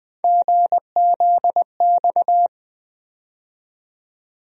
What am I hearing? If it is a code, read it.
Morse code, heard as a keyed tone, about 20 wpm, 700 Hz. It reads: GZX
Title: USA Call Sign Postfixes - 3 Letters 20wpm